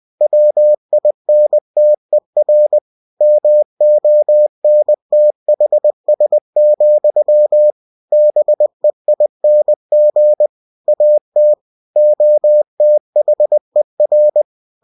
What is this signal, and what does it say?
Morse code, heard as a keyed tone, about 20 wpm, 600 Hz. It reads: WINTER MONTHS, BEING AT OTHER
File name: cw-books / winnie-the-pooh / winnie-20wpm-page-0062